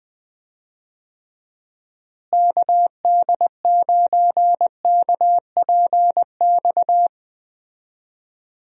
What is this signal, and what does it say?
Morse code, heard as a keyed tone, about 20 wpm, 700 Hz. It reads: KD9KPX
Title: USA Call Signs - 6 Characters - Repeated 3x - 20wpm